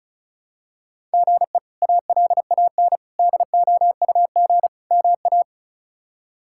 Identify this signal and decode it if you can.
Morse code, heard as a keyed tone, about 36 wpm, 700 Hz. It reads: GE ALAN DOUG MA
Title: K1USN SST QSO Practice - 35wpm